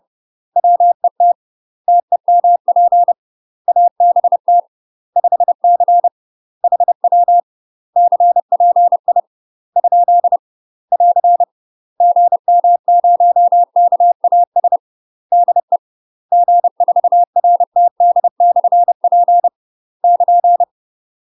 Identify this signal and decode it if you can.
Morse code, heard as a keyed tone, about 30 wpm, 700 Hz. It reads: WET TEMP ABT 5C HW CPI ? <AR> GM0KAS DE G4RTD/P <KN>
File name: UK MASTER QSOs TRAINING / 30WPM / UK MASTER QSOs 0034